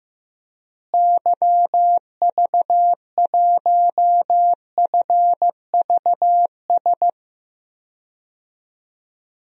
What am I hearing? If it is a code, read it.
Morse code, heard as a keyed tone, about 15 wpm, 700 Hz. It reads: YV1FVS
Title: DXCC-100 Call Signs - 6 Characters - Repeated 2x - 15wpm